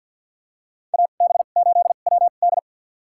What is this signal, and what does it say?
Morse code, heard as a keyed tone, about 47 wpm, spaced as for 35 wpm, 700 Hz. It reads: AB8WD
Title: USA Call Signs - 5 Characters - 2x Character Spacing - 50wpm